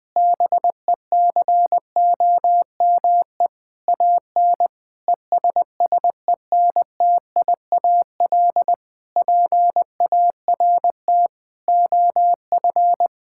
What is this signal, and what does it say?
Morse code, heard as a keyed tone, about 20 wpm, 700 Hz. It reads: BECOME AN ESSENTIAL PART OF
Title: Difficult Sentences with Hints - 20wpm